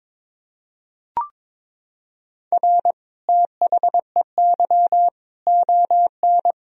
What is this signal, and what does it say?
Morse code, heard as a keyed tone, about 23 wpm, 700 Hz. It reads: R THEY ON
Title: Sentences from Top 500 Words and Abbreviations 22wpm